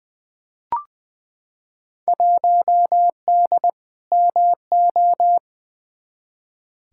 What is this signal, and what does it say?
Morse code, heard as a keyed tone, about 20 wpm, 700 Hz. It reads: 1D MO
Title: Field Day Exchanges - 20wpm